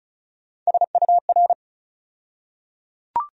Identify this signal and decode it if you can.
Morse code, heard as a keyed tone, about 36 wpm, 700 Hz. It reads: SUR
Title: USA Call Sign Postfixes - 3 Letters 35wpm